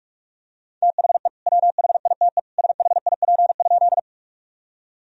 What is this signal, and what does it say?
Morse code, heard as a keyed tone, about 43 wpm, 700 Hz. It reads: THE WHITE SHIP?
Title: Sentences from Top 1000 Words - Review Part 2 - 45wpm